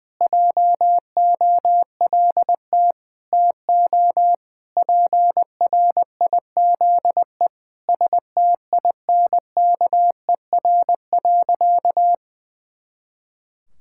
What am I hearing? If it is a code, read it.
Morse code, heard as a keyed tone, about 19 wpm, 700 Hz. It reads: JOLT TO PRIZE STINKER.
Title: Pangram Challenge v2 - 20wpm